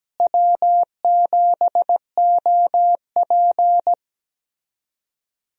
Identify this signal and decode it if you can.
Morse code, heard as a keyed tone, about 17 wpm, 700 Hz. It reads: W7OP
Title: USA Call Signs - 4 Characters - Repeated 3x - 17wpm